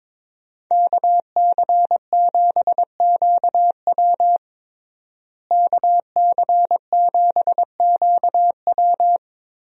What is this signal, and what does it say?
Morse code, heard as a keyed tone, about 21 wpm, 700 Hz. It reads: KC7QW KC7QW
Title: USA Call Signs - 5 Characters - Repeated 3x - 22wpm